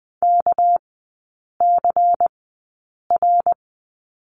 Morse code, heard as a keyed tone, about 20 wpm, spaced as for 8 wpm, 690 Hz. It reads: KCR
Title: 198 Lesson 32e - Letter k - Three Characters 20wpm